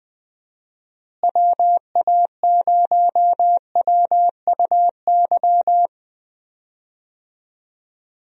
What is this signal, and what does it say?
Morse code, heard as a keyed tone, about 20 wpm, 700 Hz. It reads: WA0WUY
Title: USA Call Signs - 6 Characters - Repeated 3x - 20wpm